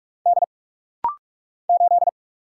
Morse code, heard as a keyed tone, about 44 wpm, 700 Hz. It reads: D 8
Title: Single Letter-Number - Rapid-Fire 45wpm